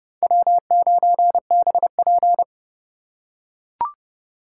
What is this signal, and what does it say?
Morse code, heard as a keyed tone, about 29 wpm, 700 Hz. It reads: W9BP